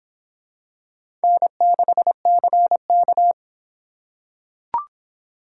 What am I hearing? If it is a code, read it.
Morse code, heard as a keyed tone, about 26 wpm, 700 Hz. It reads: N6CK